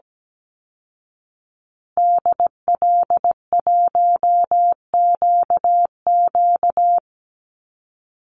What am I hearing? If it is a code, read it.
Morse code, heard as a keyed tone, about 17 wpm, 700 Hz. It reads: DL1QQ
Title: Call Signs Repeated 3x - 17wpm